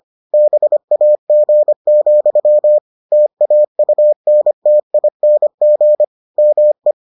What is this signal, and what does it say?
Morse code, heard as a keyed tone, about 25 wpm, 600 Hz. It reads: BAG, TAUNTING ME